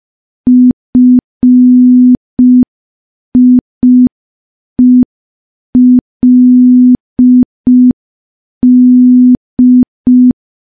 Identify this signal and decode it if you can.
Morse code, heard as a keyed tone, about 5 wpm, 250 Hz. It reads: FIELD
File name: morse / morse-training-05WPM